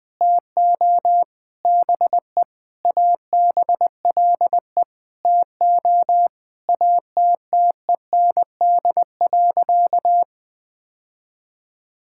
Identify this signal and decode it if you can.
Morse code, heard as a keyed tone, about 19 wpm, 700 Hz. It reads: TO BE ABLE TO ATTEND.
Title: Sentences with 7-Word Phrases - Spoken Beginning - 20wpm